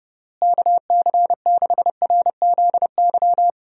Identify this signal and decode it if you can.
Morse code, heard as a keyed tone, about 30 wpm, 700 Hz. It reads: KC6RZY
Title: USA Call Signs - 6 Characters - Repeated 3x - 30wpm